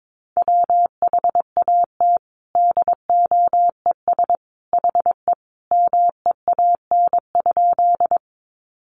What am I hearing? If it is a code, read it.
Morse code, heard as a keyed tone, about 22 wpm, 700 Hz. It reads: WHAT DOES HE MEAN?
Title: Sentences from Top 200 Words 22wpm